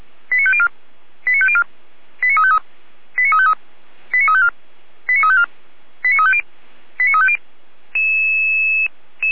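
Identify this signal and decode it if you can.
Five-tone selective call (ZVEI), heard as quick runs of five tones, tones of about 70 ms. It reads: 89504 89504 89353 89353 89354 89354 89350 89350
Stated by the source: ZVEI five-tone call received by SDR, from Feuerwehr Orlamünde's audio server.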